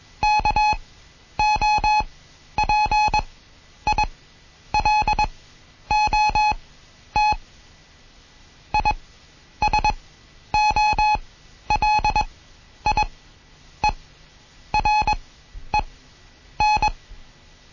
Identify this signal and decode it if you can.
Morse code, heard as a keyed tone, about 21 wpm, spaced as for 10 wpm, 850 Hz. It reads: KOPILOT ISOLIEREN
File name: Morsen / Kurs A 20